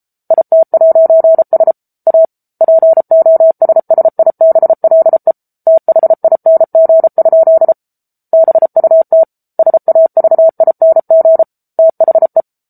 Morse code, heard as a keyed tone, about 34 wpm, 650 Hz. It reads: IT'S A POSSIBLE THING? BUT SAVING THE
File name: Book Two / 07-Chapter 7 - 33 WPM